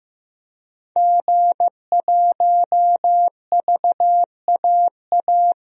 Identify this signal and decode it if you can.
Morse code, heard as a keyed tone, about 15 wpm, 700 Hz. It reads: G1VAA
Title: DXCC-100 Call Signs - 5 Characters - 15wpm